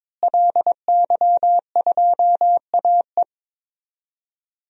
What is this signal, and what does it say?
Morse code, heard as a keyed tone, about 22 wpm, 700 Hz. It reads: LY2AE
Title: DXCC-100 Call Signs - 5 Characters - Repeated 2x - 22wpm